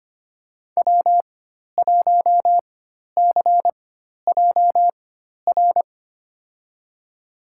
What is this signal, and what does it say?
Morse code, heard as a keyed tone, about 25 wpm, spaced as for 12 wpm, 700 Hz. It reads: W1CJR
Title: USA Call Signs - 5 Characters - 4x Character Spacing - 25wpm